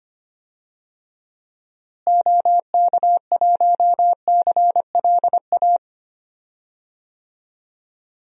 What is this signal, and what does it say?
Morse code, heard as a keyed tone, about 25 wpm, 700 Hz. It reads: OK1CLA
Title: DXCC-100 Call Signs - 6 Characters - Repeated 3x - 25wpm